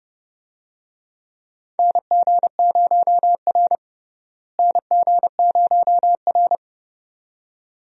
Morse code, heard as a keyed tone, about 30 wpm, 700 Hz. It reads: NG0R NG0R
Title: USA Call Signs - 4 Characters - Repeated 2x - 30wpm